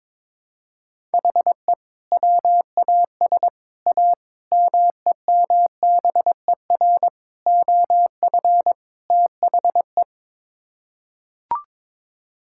Morse code, heard as a keyed tone, about 22 wpm, 700 Hz. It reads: HE WAS A MEMBER OF THE
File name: practice / 22wpm / Sets of 6 Words - 22wpm